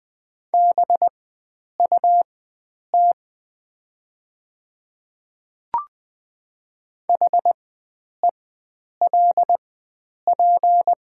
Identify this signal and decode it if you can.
Morse code, heard as a keyed tone, about 20 wpm, spaced as for 9 wpm, 700 Hz. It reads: BUT HELP
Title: 4x Character Spacing - Top 300 Words - 20wpm